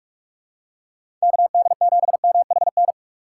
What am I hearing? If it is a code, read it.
Morse code, heard as a keyed tone, about 44 wpm, 700 Hz. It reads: KD7MHN
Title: USA Call Signs - 2x Character Spacing - 45wpm